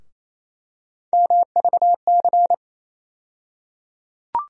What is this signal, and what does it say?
Morse code, heard as a keyed tone, about 28 wpm, 700 Hz. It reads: MVC
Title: USA Call Sign Postfixes - 3 Letters 28wpm